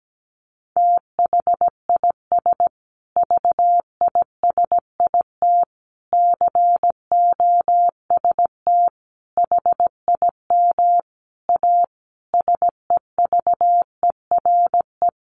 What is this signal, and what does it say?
Morse code, heard as a keyed tone, about 17 wpm, 700 Hz. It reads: THIS VISIT COST HIM A SEVERE
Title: Sentences from Top 1400 Words 17wpm